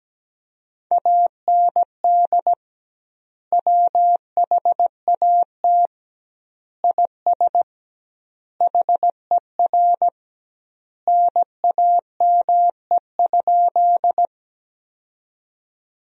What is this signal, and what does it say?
Morse code, heard as a keyed tone, about 17 wpm, 700 Hz. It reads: AND WHAT IS HER NAME?
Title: 2x Word-Spacing - Sentences from Top 200 Words - 17wpm